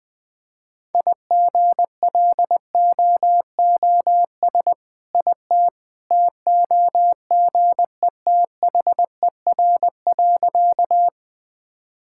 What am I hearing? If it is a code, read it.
Morse code, heard as a keyed tone, about 21 wpm, 700 Hz. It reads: IGLOOS IT TOGETHER.
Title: All-Age Appropriate Jokes - 20wpm